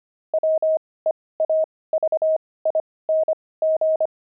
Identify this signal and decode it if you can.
Morse code, heard as a keyed tone, about 25 wpm, spaced as for 18 wpm, 630 Hz. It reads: WEAVING
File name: pangram-fs-18-cs-25-weaving